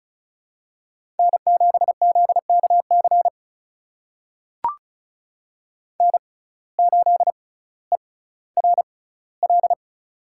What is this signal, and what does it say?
Morse code, heard as a keyed tone, about 36 wpm, spaced as for 18 wpm, 700 Hz. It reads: N7ZKC N8ERL
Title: USA Call Signs - 5 Characters - 6x Character Spacing - 35wpm